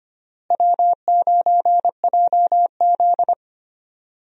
Morse code, heard as a keyed tone, about 25 wpm, 700 Hz. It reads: W9JZ
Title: USA Call Signs - 4 Characters - Repeated 2x - 25wpm